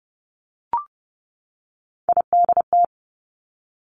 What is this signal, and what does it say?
Morse code, heard as a keyed tone, about 30 wpm, 700 Hz. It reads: IDT
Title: Three Challenging Letter-Number Combinations 30wpm